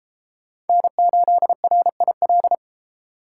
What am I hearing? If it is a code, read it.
Morse code, heard as a keyed tone, about 33 wpm, 700 Hz. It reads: N8RIL